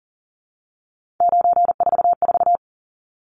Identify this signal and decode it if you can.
Morse code, heard as a keyed tone, about 40 wpm, 700 Hz. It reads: 944
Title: Numbers - 3-Digit 40wpm